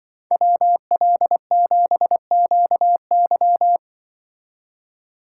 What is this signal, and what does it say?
Morse code, heard as a keyed tone, about 24 wpm, 700 Hz. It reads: WL7QY
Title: Speed Racing - USA 2x2 Call Signs 20wpm